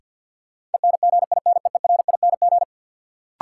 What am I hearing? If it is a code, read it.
Morse code, heard as a keyed tone, about 51 wpm, 700 Hz. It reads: ENGINEERING
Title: Sentences from Top 1500 Words - New Word Drill Alt - 50wpm